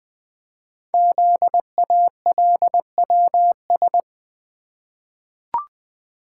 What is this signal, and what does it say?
Morse code, heard as a keyed tone, about 20 wpm, 700 Hz. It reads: ZALWS
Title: Code Groups - 5 Letters - 20wpm